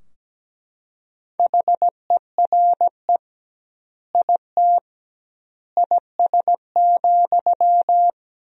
Morse code, heard as a keyed tone, about 17 wpm, 700 Hz. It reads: HERE IT IS,
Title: 2x Word-Spacing - Sentences from Top 200 Words - 17wpm